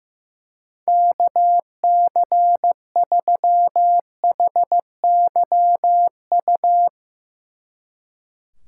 Morse code, heard as a keyed tone, about 17 wpm, 700 Hz. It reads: KC3HYU
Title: Speed Racing - USA 2x3 Call Signs 15wpm